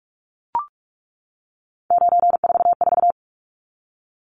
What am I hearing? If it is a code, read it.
Morse code, heard as a keyed tone, about 44 wpm, 700 Hz. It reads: 944
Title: Numbers - 3-Digit 45wpm